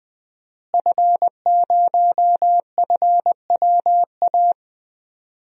Morse code, heard as a keyed tone, about 20 wpm, 700 Hz. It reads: F0FWA
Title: DXCC-100 Call Signs - 5 Characters - Repeated 2x - 20wpm